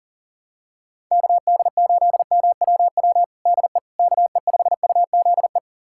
Morse code, heard as a keyed tone, about 40 wpm, 700 Hz. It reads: KD8MWW DE KE5UZE
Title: USA Callsign DE Callsign - 6 Characters - 40wpm